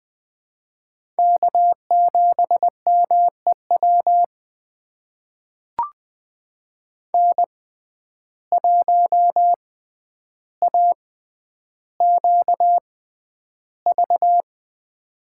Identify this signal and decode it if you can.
Morse code, heard as a keyed tone, about 20 wpm, spaced as for 10 wpm, 700 Hz. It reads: K7MEW N1AQV
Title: USA Call Signs - 5 Characters - 6x Character Spacing - 20wpm